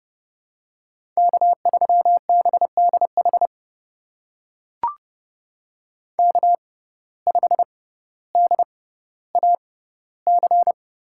Morse code, heard as a keyed tone, about 30 wpm, spaced as for 15 wpm, 700 Hz. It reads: K3BDH K5DAC
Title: USA Call Signs - 5 Characters - 6x Character Spacing - 30wpm